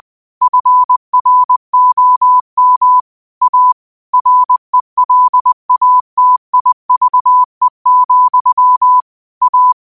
Morse code, heard as a keyed tone, about 20 wpm, 1000 Hz. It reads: FROM A RELATIVE, A
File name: qotdfiles / 20wpm